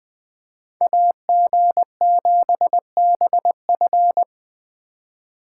Morse code, heard as a keyed tone, about 20 wpm, 700 Hz. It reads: AG7BF
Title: Speed Racing - USA 2x2 Call Signs 17wpm